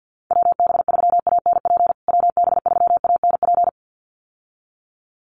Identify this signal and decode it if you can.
Morse code, heard as a keyed tone, about 48 wpm, 700 Hz. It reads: WB3ANP WB3ANP
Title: USA Call Signs Repeated 2x 50wpm